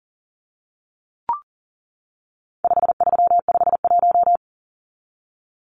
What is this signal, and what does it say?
Morse code, heard as a keyed tone, about 40 wpm, 700 Hz. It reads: 5351
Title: Numbers - 4-Digit 40wpm